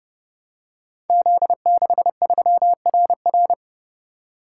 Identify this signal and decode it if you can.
Morse code, heard as a keyed tone, about 32 wpm, 700 Hz. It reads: Z63RR
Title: Code Groups - 5 Letter:Numbers - 30wpm